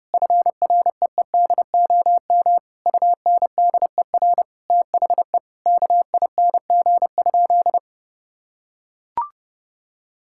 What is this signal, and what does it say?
Morse code, heard as a keyed tone, about 28 wpm, 700 Hz. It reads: FREEDOM UNDER THE KING?
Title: Sentences from Top 1000 Words - Review Part 6 - 30wpm